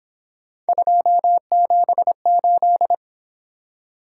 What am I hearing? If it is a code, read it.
Morse code, heard as a keyed tone, about 26 wpm, 700 Hz. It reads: 278
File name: Speed-Racing - Three Letter-Number Combinations - 22wpm